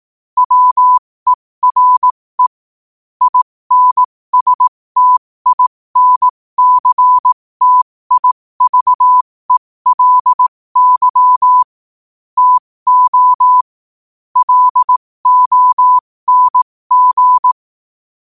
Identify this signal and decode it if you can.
Morse code, heard as a keyed tone, about 18 wpm, spaced as for 15 wpm, 1000 Hz. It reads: WERE INSTINCTIVELY TO LONG
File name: qotdfiles / 15wpm